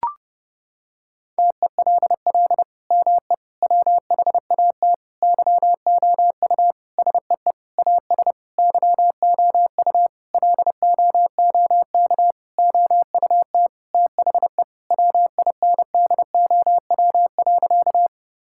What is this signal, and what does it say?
Morse code, heard as a keyed tone, about 32 wpm, 700 Hz. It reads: TELL ME WHAT YOU SEE AS YOU LOOK OUT THE WINDOW.
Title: Sentences from Top 1600 Words 30wpm